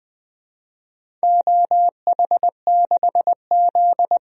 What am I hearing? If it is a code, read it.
Morse code, heard as a keyed tone, about 20 wpm, 700 Hz. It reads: OH6Z
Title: DXCC-100 Call Signs - 4 Characters - 20wpm